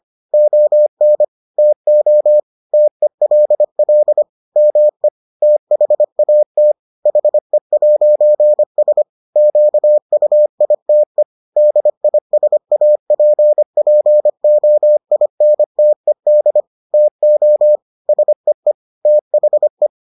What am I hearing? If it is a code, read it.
Morse code, heard as a keyed tone, about 25 wpm, 600 Hz. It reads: ON TO TELL ME THAT HE'S QUITE DISAPPOINTED TO SEE THE